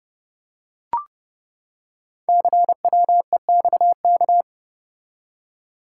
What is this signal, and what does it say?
Morse code, heard as a keyed tone, about 30 wpm, 700 Hz. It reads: CWEXK
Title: Code Groups - 5 Letters - 30wpm